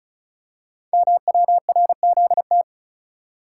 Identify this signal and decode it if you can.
Morse code, heard as a keyed tone, about 36 wpm, 690 Hz. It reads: MWRZT
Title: Code Groups - 5 Letters - 35wpm